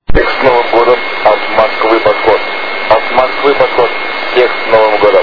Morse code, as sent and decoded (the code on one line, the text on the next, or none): none